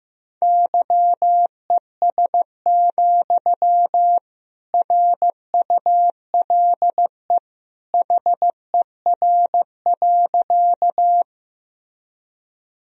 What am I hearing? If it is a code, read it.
Morse code, heard as a keyed tone, about 15 wpm, 700 Hz. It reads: YES, RULE HER.
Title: Sentences from Top 1000 Words - Review Part 2 - 15wpm